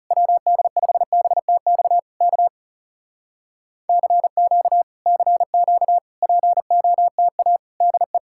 Morse code, heard as a keyed tone, about 40 wpm, 700 Hz. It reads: WD5BTX K CQ CQ POTA DE